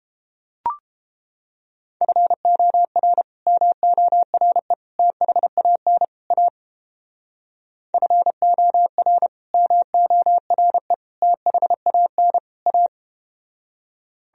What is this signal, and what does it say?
Morse code, heard as a keyed tone, about 31 wpm, 700 Hz. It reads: FOR MORE THAN A FOR MORE THAN A